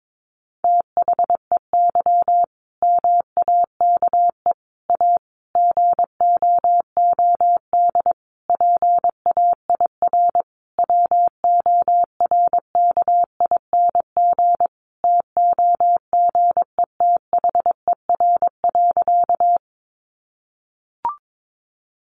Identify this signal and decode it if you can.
Morse code, heard as a keyed tone, about 22 wpm, 700 Hz. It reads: THEY MAKE A GOOD PAIR WORKING TOGETHER.
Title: Sentences from Top 1200 Words 22wpm